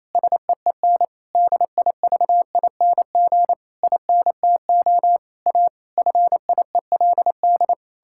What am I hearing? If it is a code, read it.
Morse code, heard as a keyed tone, about 28 wpm, 700 Hz. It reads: SEEN DIVING INTO A FIELD
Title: Pangram Challenge v2 - 28wpm